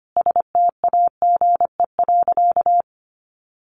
Morse code, heard as a keyed tone, about 25 wpm, 700 Hz. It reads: STAGE.
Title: Sentences from Top 1200 Words 25wpm